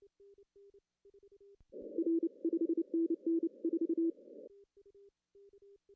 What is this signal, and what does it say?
Morse code, heard as a keyed tone, about 29 wpm, 330 Hz. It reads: R 5NN 4
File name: CW / cqww-cw-k5tr